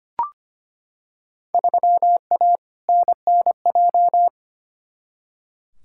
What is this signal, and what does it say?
Morse code, heard as a keyed tone, about 25 wpm, 700 Hz. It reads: 3A NNJ
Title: Field Day Exchanges - 25wpm